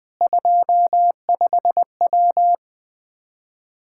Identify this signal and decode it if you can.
Morse code, heard as a keyed tone, about 21 wpm, 700 Hz. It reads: 25W
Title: Practice QSO - Exchange 2 - Info - 20wpm